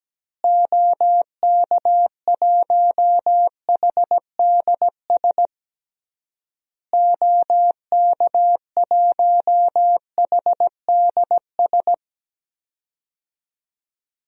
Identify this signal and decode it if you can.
Morse code, heard as a keyed tone, about 17 wpm, 700 Hz. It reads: OK1HDS OK1HDS
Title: DXCC-100 Call Signs - 6 Characters - Repeated 3x - 17wpm